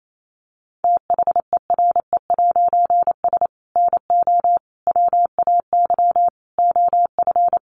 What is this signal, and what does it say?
Morse code, heard as a keyed tone, about 28 wpm, 700 Hz. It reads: THERE'S NO WAY OF
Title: A Sound Of Thunder 28wpm